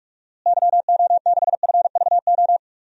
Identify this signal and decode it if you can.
Morse code, heard as a keyed tone, about 46 wpm, 700 Hz. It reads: YO6FVO
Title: DXCC-100 Call Signs - 6 Characters - Repeated 3x - 45wpm